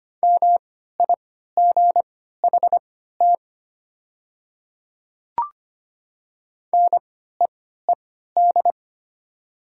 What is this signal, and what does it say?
Morse code, heard as a keyed tone, about 25 wpm, spaced as for 14 wpm, 700 Hz. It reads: MIGHT NEED
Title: 3x Character Spacing - Top 500 Words - 25wpm